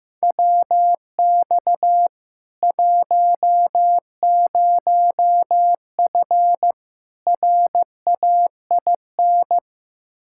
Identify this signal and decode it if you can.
Morse code, heard as a keyed tone, about 15 wpm, 700 Hz. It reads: WX 10F RAIN
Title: K1USN SST QSO Practice 15wpm